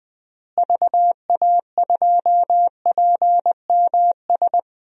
Morse code, heard as a keyed tone, about 20 wpm, 700 Hz. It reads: VA2PMS
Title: DXCC-100 Call Signs - 6 Characters - Repeated 2x - 20wpm